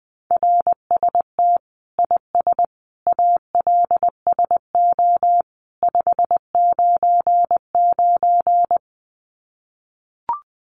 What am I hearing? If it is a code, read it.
Morse code, heard as a keyed tone, about 20 wpm, 700 Hz. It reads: RST IS ALSO 599
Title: Common QSO Phrases 20wpm